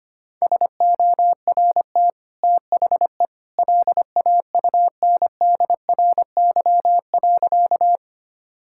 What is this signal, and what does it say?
Morse code, heard as a keyed tone, about 25 wpm, 700 Hz. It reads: SORT THE LAUNDRY.
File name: Easy Sentences with Hints - 25wpm